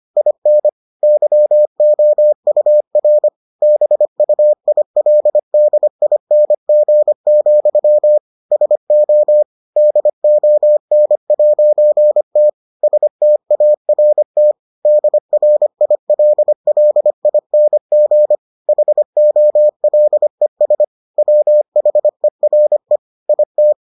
Morse code, heard as a keyed tone, about 25 wpm, 600 Hz. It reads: IN YOUR BUILDING, SO DON'T START DRILLING HOLES WHERE IT